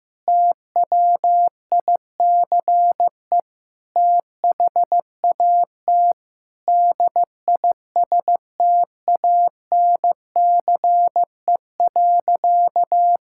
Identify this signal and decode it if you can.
Morse code, heard as a keyed tone, about 16 wpm, 700 Hz. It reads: TWICE THAT DISTANCE.
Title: Sentences from Top 1600 Words 15wpm